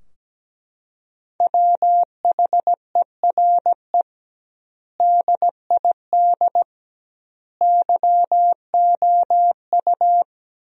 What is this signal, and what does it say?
Morse code, heard as a keyed tone, about 17 wpm, 700 Hz. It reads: WHERE DID YOU
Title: 2x Word-Spacing - Sentences from Top 200 Words - 17wpm